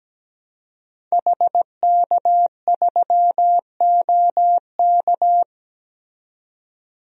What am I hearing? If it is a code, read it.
Morse code, heard as a keyed tone, about 17 wpm, 700 Hz. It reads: HK3OK
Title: DXCC-100 Call Signs - 5 Characters - 17wpm